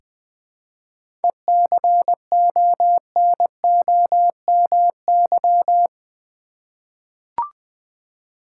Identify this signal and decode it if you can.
Morse code, heard as a keyed tone, about 20 wpm, 700 Hz. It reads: ECONOMY
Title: Seven Letter Words - 20wpm